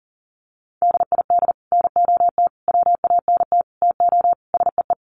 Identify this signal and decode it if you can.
Morse code, heard as a keyed tone, about 40 wpm, 700 Hz. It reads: DID NOT WANT TO SEE
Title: Sets of 5 Words Encore2 40wpm